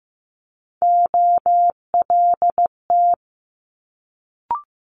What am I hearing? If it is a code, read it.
Morse code, heard as a keyed tone, about 15 wpm, 700 Hz. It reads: OLT
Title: Multi-letter Phonemes Quick Fire 15wpm